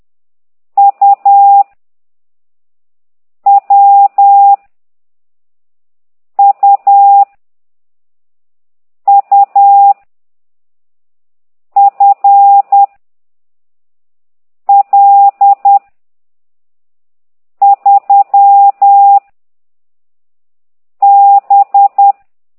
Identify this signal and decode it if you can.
Morse code, heard as a keyed tone, about 10 wpm, 800 Hz. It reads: U W U U F L 3 B
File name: ct2idw / exerc15